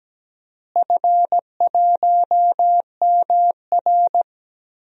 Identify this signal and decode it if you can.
Morse code, heard as a keyed tone, about 17 wpm, 700 Hz. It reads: F1MR
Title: DXCC-100 Call Signs - 4 Characters - Repeated 2x - 17wpm